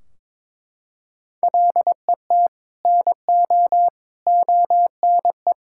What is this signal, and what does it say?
Morse code, heard as a keyed tone, about 22 wpm, 700 Hz. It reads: LET NO ONE
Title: Sentences from Top 1600 Words 22wpm